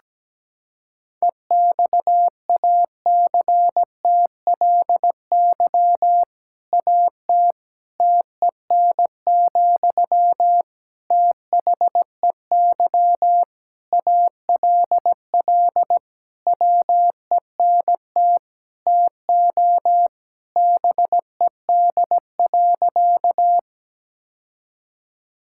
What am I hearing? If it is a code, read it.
Morse code, heard as a keyed tone, about 17 wpm, 700 Hz. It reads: EXACTLY AT TEN, THEY ALL WENT TO BED.
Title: Sentences from Top 1000 Words - Review Part 6 - 17wpm